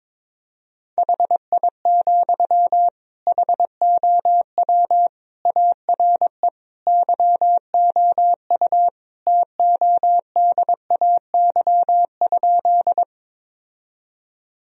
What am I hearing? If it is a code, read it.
Morse code, heard as a keyed tone, about 21 wpm, 700 Hz. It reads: HI, HOW ARE YOU TODAY?
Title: Sentences from Top 1000 Words - Review Part 3 - 22wpm